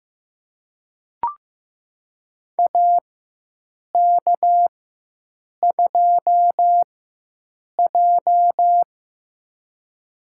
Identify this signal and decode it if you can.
Morse code, heard as a keyed tone, about 15 wpm, spaced as for 7 wpm, 700 Hz. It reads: AK2J
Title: USA Call Signs - 4x Character Spacing - 15wpm